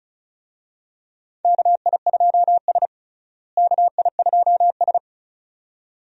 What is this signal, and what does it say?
Morse code, heard as a keyed tone, about 36 wpm, 700 Hz. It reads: KI2S KI2S
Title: USA Call Signs - 4 Characters - Repeated 2x - 35wpm